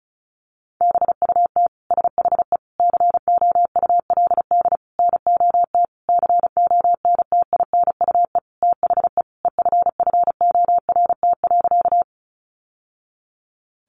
Morse code, heard as a keyed tone, about 34 wpm, 700 Hz. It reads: BUT SHE COULD NOT CONTINUE THE EFFORT.
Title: Sentences from Top 1200 Words 35wpm